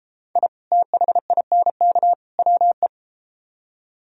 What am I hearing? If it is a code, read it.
Morse code, heard as a keyed tone, about 32 wpm, 700 Hz. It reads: I THINK WE